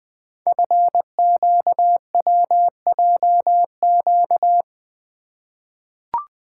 Morse code, heard as a keyed tone, about 20 wpm, 700 Hz. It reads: FQWJQ
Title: Code Groups - 5 Letters - 20wpm